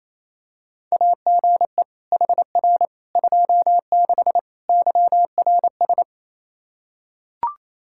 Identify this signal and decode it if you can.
Morse code, heard as a keyed tone, about 28 wpm, 700 Hz. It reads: AGE HR 26 YRS
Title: Practice QSO - Exchange 3 - More Info - 28wpm